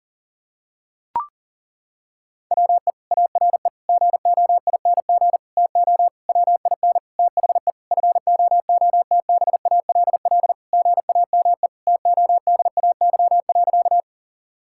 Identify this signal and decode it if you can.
Morse code, heard as a keyed tone, about 37 wpm, 700 Hz. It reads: WE ARE GOING TO WIN THE FOOTBALL GAME TODAY.
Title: Sentences from Top 1600 Words 40wpm